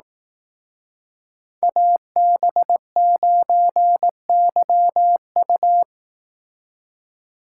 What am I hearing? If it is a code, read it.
Morse code, heard as a keyed tone, about 18 wpm, 700 Hz. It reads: AB9YU
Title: Speed Racing - USA 2x2 Call Signs 15wpm